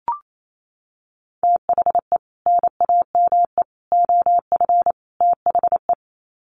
Morse code, heard as a keyed tone, about 28 wpm, 700 Hz. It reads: THE NAME OF THE
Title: Sets of 4 Words 28wpm